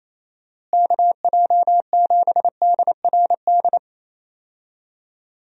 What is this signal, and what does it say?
Morse code, heard as a keyed tone, about 28 wpm, 700 Hz. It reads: KJ7DRD
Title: USA Call Signs - 6 Characters - Repeated 2x - 28wpm